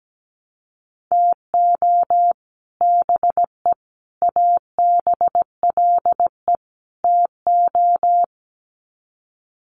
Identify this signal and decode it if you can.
Morse code, heard as a keyed tone, about 17 wpm, 700 Hz. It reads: TO BE ABLE TO
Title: Sets of 4 Words 17wpm